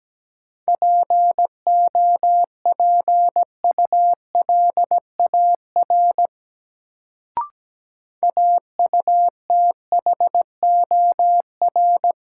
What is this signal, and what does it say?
Morse code, heard as a keyed tone, about 17 wpm, 700 Hz. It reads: POPULAR AUTHOR
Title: Sentences from Top 1200 Words - New Word Drill Alt - 17wpm